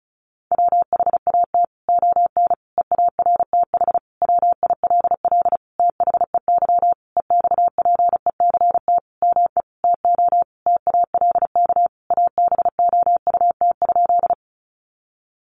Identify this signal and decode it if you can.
Morse code, heard as a keyed tone, about 34 wpm, 700 Hz. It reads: WHAT ON EARTH WILL THEY EXPECT ME TO TALK ABOUT?
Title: Sentences from Top 1100 Words 35wpm